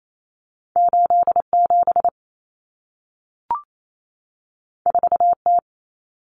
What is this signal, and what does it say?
Morse code, heard as a keyed tone, about 28 wpm, 700 Hz. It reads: 87 4T
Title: Common Cut Numbers - 2-Digit 28wpm